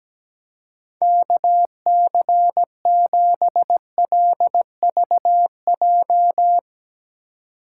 Morse code, heard as a keyed tone, about 17 wpm, 700 Hz. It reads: KC7LVJ
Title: Speed Racing - USA 2x3 Call Signs 17wpm